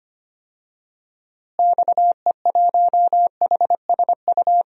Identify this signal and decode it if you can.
Morse code, heard as a keyed tone, about 25 wpm, 700 Hz. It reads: XE1HSU
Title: DXCC-100 Call Signs - 6 Characters - Repeated 3x - 25wpm